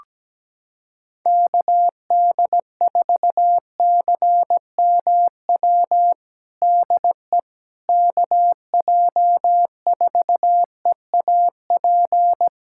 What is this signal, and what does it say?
Morse code, heard as a keyed tone, about 17 wpm, 700 Hz. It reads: KD4CMW DE KJ4EAP
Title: USA Callsign DE Callsign 17wpm